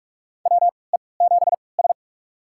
Morse code, heard as a keyed tone, about 48 wpm, spaced as for 26 wpm, 700 Hz. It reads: WE7S
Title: USA Call Signs - 4 Characters - 3x Character Spacing - 45wpm